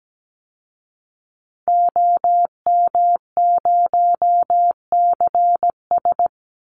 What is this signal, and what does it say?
Morse code, heard as a keyed tone, about 17 wpm, 700 Hz. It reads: OM0CS
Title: Call Signs Repeated 2x - 17wpm